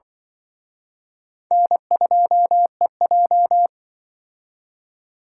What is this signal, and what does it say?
Morse code, heard as a keyed tone, about 26 wpm, 700 Hz. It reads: N2EJ